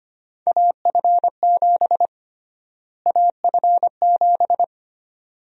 Morse code, heard as a keyed tone, about 25 wpm, 700 Hz. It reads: AF7 AF7
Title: USA Call Sign Prefixes - Repeated 2x - 25wpm